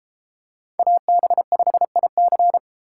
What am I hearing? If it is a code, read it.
Morse code, heard as a keyed tone, about 32 wpm, 700 Hz. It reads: AB5IC